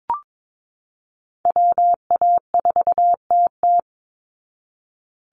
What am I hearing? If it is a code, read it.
Morse code, heard as a keyed tone, about 22 wpm, 700 Hz. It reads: WA4TT
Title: Call Signs Repeated 2x - 22wpm